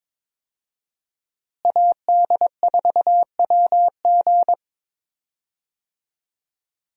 Morse code, heard as a keyed tone, about 22 wpm, 700 Hz. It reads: AD4WG
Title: USA Call Signs - 5 Characters - Repeated 2x - 22wpm